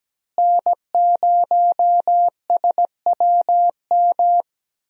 Morse code, heard as a keyed tone, about 17 wpm, 700 Hz. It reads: N0SWM
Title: USA Call Signs - 5 Characters - Repeated 2x - 17wpm